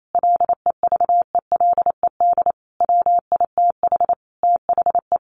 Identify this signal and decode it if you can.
Morse code, heard as a keyed tone, about 28 wpm, 700 Hz. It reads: LEVELED WITH THE
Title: A Sound Of Thunder 28wpm